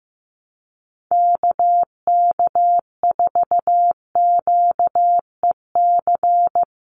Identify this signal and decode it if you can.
Morse code, heard as a keyed tone, about 15 wpm, 700 Hz. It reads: KK4QEC
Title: USA Call Signs 15wpm